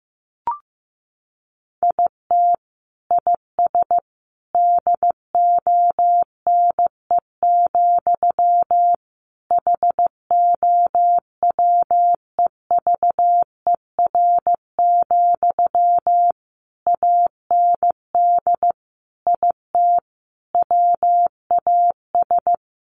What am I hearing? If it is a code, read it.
Morse code, heard as a keyed tone, about 15 wpm, 700 Hz. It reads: IT IS DONE, HOWEVER, AND IT WAS
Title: Sentences from Top 600 Words 15wpm